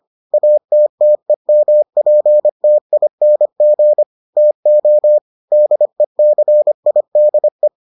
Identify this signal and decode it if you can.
Morse code, heard as a keyed tone, about 25 wpm, 600 Hz. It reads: ATTEMPTING TO DECIDE